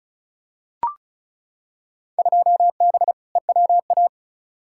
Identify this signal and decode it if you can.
Morse code, heard as a keyed tone, about 36 wpm, 690 Hz. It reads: 2B EWA
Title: Field Day Exchanges Encore - 35wpm